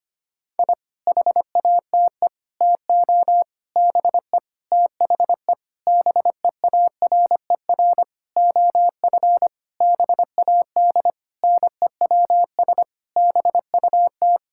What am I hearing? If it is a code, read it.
Morse code, heard as a keyed tone, about 25 wpm, 700 Hz. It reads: I HATE TO BE THE BEARER OF BAD NEWS BUT
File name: practice / 25wpm / Sentences with 9-Word Phrases - 25wpm